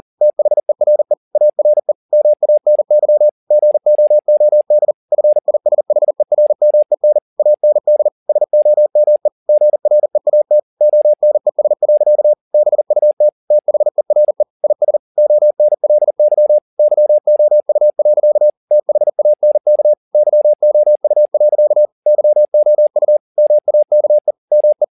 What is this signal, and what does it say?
Morse code, heard as a keyed tone, about 40 wpm, 600 Hz. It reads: THERE ARE MANY GOOD FISHERMEN AND SOME GREAT ONES. BUT THERE IS ONLY YOU. THANK YOU. YOU MAKE ME